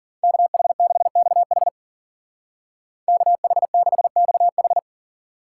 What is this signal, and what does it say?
Morse code, heard as a keyed tone, about 49 wpm, 700 Hz. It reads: KH6XH KH6XH
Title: Speed Racing - USA 2x2 Call Signs 40wpm